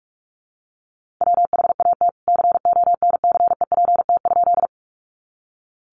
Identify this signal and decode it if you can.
Morse code, heard as a keyed tone, about 44 wpm, 700 Hz. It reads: WHAT CONCEPT?
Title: Sentences from Top 1100 Words 45wpm